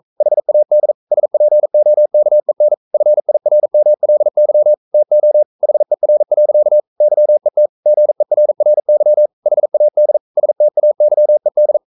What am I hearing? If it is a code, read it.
Morse code, heard as a keyed tone, about 42 wpm, 600 Hz. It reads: HAD SPOKEN FIRMLY TO HER. YET GERRY HAD STAYED
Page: Romanen Never af Ken Follet på engelsk.